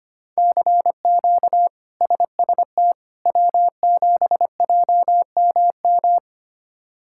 Morse code, heard as a keyed tone, about 25 wpm, 700 Hz. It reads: CQ SST W7JMM
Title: K1USN SST QSO Practice - 25wpm